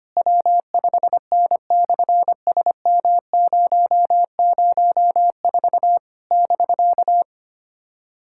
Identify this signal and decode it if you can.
Morse code, heard as a keyed tone, about 24 wpm, 700 Hz. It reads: W5N/SM004 <BK>
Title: SOTA QSO Practice 25wpm